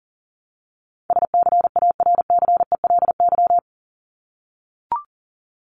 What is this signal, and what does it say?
Morse code, heard as a keyed tone, about 40 wpm, 700 Hz. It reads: SCARCELY
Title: LY Suffix - Top 100 Words with Base Words 40wpm